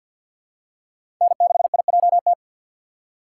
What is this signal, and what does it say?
Morse code, heard as a keyed tone, about 51 wpm, 700 Hz. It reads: N6IJT
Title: USA Call Signs - 5 Characters - 2x Character Spacing - 50wpm